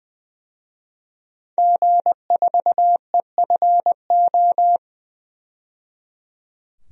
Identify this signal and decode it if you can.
Morse code, heard as a keyed tone, about 20 wpm, 700 Hz. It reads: G4EFO
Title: DXCC-100 Call Signs - 5 Characters - 20wpm